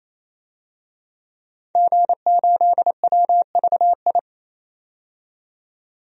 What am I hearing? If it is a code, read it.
Morse code, heard as a keyed tone, about 28 wpm, 700 Hz. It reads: G8WVI
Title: DXCC-100 Call Signs - 5 Characters - 28wpm